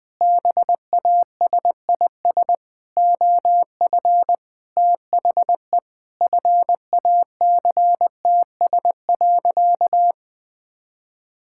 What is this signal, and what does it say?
Morse code, heard as a keyed tone, about 20 wpm, 700 Hz. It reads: BASIS OF THE FACTS.
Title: Sentences with 5-Word Phrases - 20wpm